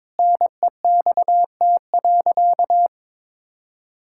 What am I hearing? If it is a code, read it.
Morse code, heard as a keyed tone, about 22 wpm, 700 Hz. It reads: NEXT.
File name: practice / 22wpm / Sentences with 8-Word Phrases - 22wpm